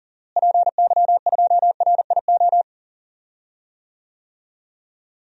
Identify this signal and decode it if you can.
Morse code, heard as a keyed tone, about 38 wpm, 700 Hz. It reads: PY2RIO
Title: DXCC-100 Call Signs - 6 Characters - Repeated 3x - 40wpm